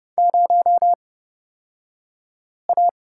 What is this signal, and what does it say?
Morse code, heard as a keyed tone, about 30 wpm, spaced as for 5 wpm, 700 Hz. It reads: 0A
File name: Lesson 37d - Letter x - Two Characters 30wpm